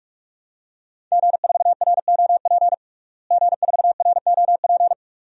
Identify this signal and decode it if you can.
Morse code, heard as a keyed tone, about 46 wpm, 700 Hz. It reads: G4ROP G4ROP
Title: DXCC-100 Call Signs - 5 Characters - Repeated 2x - 45wpm